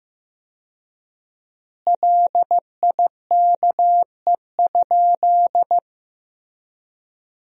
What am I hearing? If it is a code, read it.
Morse code, heard as a keyed tone, about 15 wpm, 700 Hz. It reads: LIKE?
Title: 4x Word-Spacing - Sentences from Top 100 Words - 15wpm